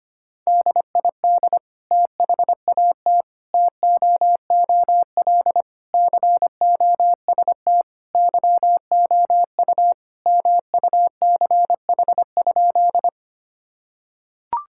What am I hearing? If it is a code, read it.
Morse code, heard as a keyed tone, about 25 wpm, 700 Hz. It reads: DID THAT TOOL COST YOU MUCH?
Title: Sentences from Top 1000 Words - Review Part 6 - 25wpm